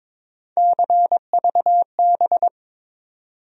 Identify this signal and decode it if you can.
Morse code, heard as a keyed tone, about 22 wpm, 700 Hz. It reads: CVB
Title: USA Call Sign Postfixes - 3 Letters 22wpm